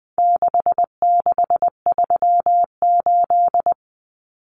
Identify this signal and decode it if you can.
Morse code, heard as a keyed tone, about 20 wpm, 700 Hz. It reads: 6638
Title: Numbers - 4-Digit 20wpm